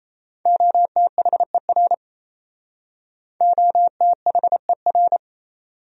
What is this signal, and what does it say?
Morse code, heard as a keyed tone, about 37 wpm, 700 Hz. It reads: OTHER OTHER